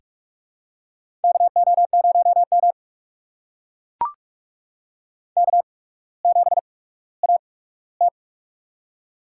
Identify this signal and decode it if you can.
Morse code, heard as a keyed tone, about 44 wpm, spaced as for 19 wpm, 700 Hz. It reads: KO0M K7AT
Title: USA Call Signs - 4 Characters - 8x Character Spacing - 45wpm